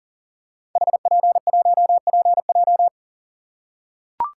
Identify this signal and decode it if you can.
Morse code, heard as a keyed tone, about 38 wpm, 690 Hz. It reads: HP1PJ
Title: Code Groups - 5 Letter:Numbers - 40wpm